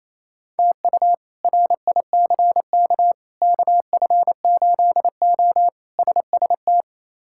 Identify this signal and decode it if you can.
Morse code, heard as a keyed tone, about 28 wpm, 700 Hz. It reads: TU RICK KF8O SST
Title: K1USN SST QSO Practice - 28wpm